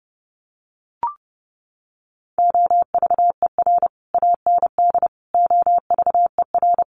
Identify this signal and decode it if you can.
Morse code, heard as a keyed tone, about 30 wpm, 700 Hz. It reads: OVER AND OVER
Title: Sets of 4 Words 30wpm